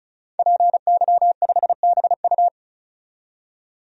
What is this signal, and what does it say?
Morse code, heard as a keyed tone, about 36 wpm, 700 Hz. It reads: PY5BU
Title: DXCC-100 Call Signs - 5 Characters - 35wpm